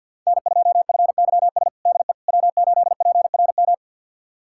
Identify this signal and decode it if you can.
Morse code, heard as a keyed tone, about 50 wpm, 700 Hz. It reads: N2FYS DE W8PRM
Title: USA Callsign DE Callsign - 5 Characters - 50wpm